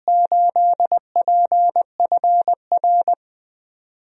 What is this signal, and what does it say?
Morse code, heard as a keyed tone, about 20 wpm, 700 Hz. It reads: 8PFR
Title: DXCC-100 Call Signs - 6 Characters - Repeated 3x - 20wpm